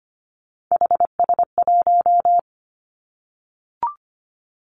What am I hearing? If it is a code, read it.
Morse code, heard as a keyed tone, about 25 wpm, 700 Hz. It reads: HS1
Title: Three Challenging Letter-Number Combinations 25wpm